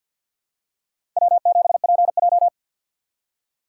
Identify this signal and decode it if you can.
Morse code, heard as a keyed tone, about 46 wpm, 700 Hz. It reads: W7PJ
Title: USA Call Signs - 4 Characters - 2x Character Spacing - 50wpm